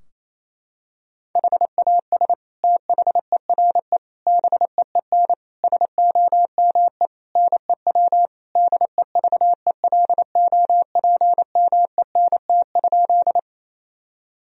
Morse code, heard as a keyed tone, about 28 wpm, 700 Hz. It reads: HAS THERE BEEN SOME NEW DEVELOPMENT?
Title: Sentences from Top 1000 Words - Review Part 4 - 28wpm